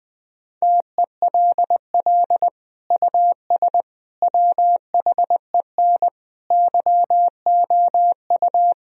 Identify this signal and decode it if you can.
Morse code, heard as a keyed tone, about 21 wpm, 700 Hz. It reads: TELL US WHEN YOU
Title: Sentences from Top 1000 Words - Review Part 2 - 20wpm